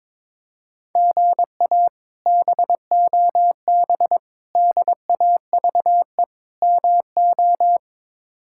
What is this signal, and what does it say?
Morse code, heard as a keyed tone, about 21 wpm, 700 Hz. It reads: GA BOB DAVE MO
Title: K1USN SST QSO Practice - 22wpm